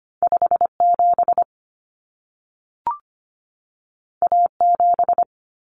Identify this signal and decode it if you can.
Morse code, heard as a keyed tone, about 25 wpm, 700 Hz. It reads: 57 A7
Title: Common Cut Numbers - 2-Digit 25wpm